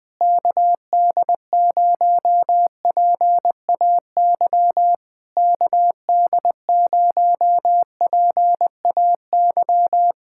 Morse code, heard as a keyed tone, about 20 wpm, 700 Hz. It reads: KD0PAY KD0PAY
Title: Speed-Racing Common QSO Phrases - 20wpm